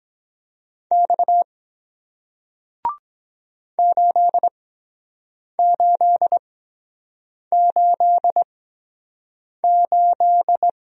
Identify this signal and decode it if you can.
Morse code, heard as a keyed tone, about 23 wpm, spaced as for 7 wpm, 700 Hz. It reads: X 8888